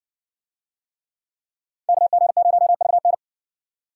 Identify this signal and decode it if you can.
Morse code, heard as a keyed tone, about 60 wpm, 700 Hz. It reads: KG0VN